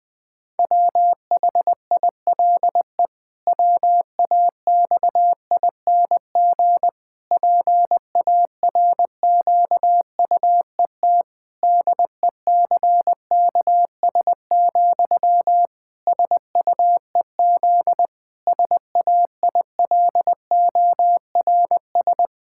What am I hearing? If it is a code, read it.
Morse code, heard as a keyed tone, about 20 wpm, 700 Hz. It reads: WHILE WAXING PARQUET DECKS, SUEZ SAILORS
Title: Pangram Challenge v2 - 20wpm